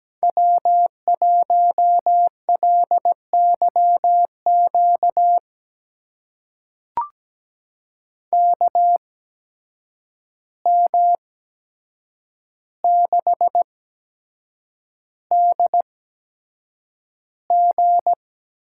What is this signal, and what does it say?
Morse code, heard as a keyed tone, about 17 wpm, spaced as for 7 wpm, 700 Hz. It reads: W1LYQ KM6DG
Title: USA Call Signs - 5 Characters - 8x Character Spacing - 17wpm